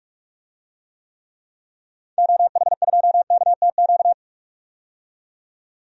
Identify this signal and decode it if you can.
Morse code, heard as a keyed tone, about 45 wpm, 700 Hz. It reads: OH2KTQ
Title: DXCC-100 Call Signs - 6 Characters - 45wpm